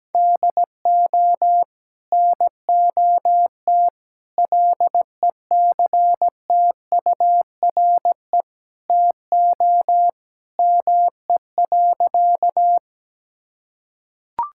Morse code, heard as a keyed tone, about 17 wpm, 700 Hz. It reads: DO NOT LECTURE TO ME.
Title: Sentences from Top 1600 Words 17wpm